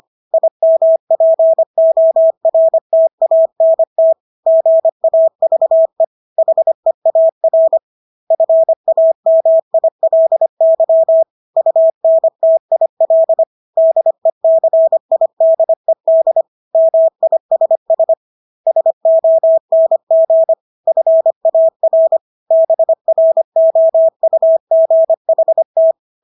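Morse code, heard as a keyed tone, about 25 wpm, 650 Hz. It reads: IMPORTANT GAVE HEAR FAMILY UNTIL DECIDED MISS SONG FAR BROUGHT
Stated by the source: Morse practice of English words at 25 wpm from CW med Gnister.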